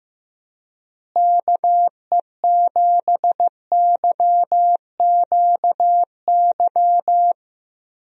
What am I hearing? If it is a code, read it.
Morse code, heard as a keyed tone, about 15 wpm, 700 Hz. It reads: KE7YQY
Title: USA Call Signs - 6 Characters - Repeated 3x - 15wpm